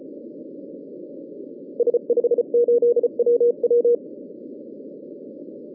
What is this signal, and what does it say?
Morse code, heard as a keyed tone, about 34 wpm, 480 Hz. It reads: S58WW
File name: 2011 bcc-challenge samstag tape